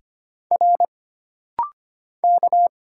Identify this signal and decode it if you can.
Morse code, heard as a keyed tone, about 25 wpm, 700 Hz. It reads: R K
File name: Single Character (Letter:Number:Punctuation:Prosign) - Rapid-Fire - 25wpm